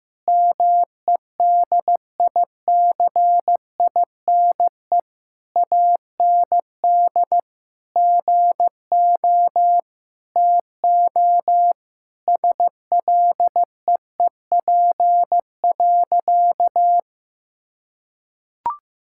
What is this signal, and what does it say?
Morse code, heard as a keyed tone, about 16 wpm, 700 Hz. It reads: MEDICINE AND GO TO SLEEP.
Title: Sentences from Top 1000 Words - Review Part 6 - 15wpm